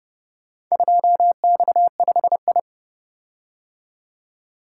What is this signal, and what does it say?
Morse code, heard as a keyed tone, about 29 wpm, 700 Hz. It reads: 2X5I